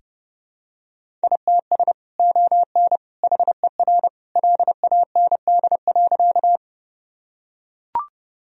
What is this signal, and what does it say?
Morse code, heard as a keyed tone, about 30 wpm, 700 Hz. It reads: ITS ON HER LAND.
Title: Sentences from Top 1000 Words - Review Part 1 - 30wpm